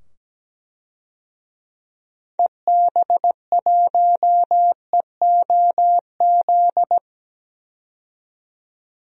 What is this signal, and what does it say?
Morse code, heard as a keyed tone, about 17 wpm, 700 Hz. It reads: EB1EOZ
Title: DXCC-100 Call Signs - 6 Characters - 17wpm